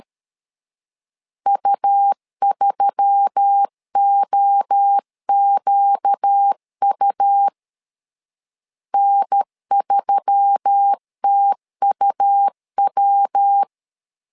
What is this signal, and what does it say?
Morse code, heard as a keyed tone, about 13 wpm, 800 Hz. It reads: U3OQU N3TUW
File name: LEZIONI / 4 Q1S3 12wpm 2023